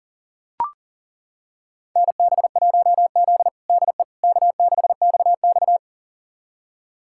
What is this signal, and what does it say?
Morse code, heard as a keyed tone, about 40 wpm, 700 Hz. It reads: NB1Z DE K6XX
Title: USA Callsign DE Callsign - 4 Characters - 40wpm